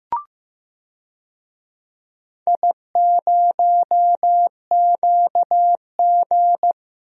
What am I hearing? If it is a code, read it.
Morse code, heard as a keyed tone, about 15 wpm, 700 Hz. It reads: I0QG
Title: DXCC-100 Call Signs - 4 Characters - Repeated 2x - 15wpm